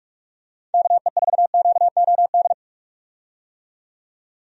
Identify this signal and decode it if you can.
Morse code, heard as a keyed tone, about 46 wpm, 700 Hz. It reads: KE4QOD
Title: USA Call Signs - 6 Characters - 45wpm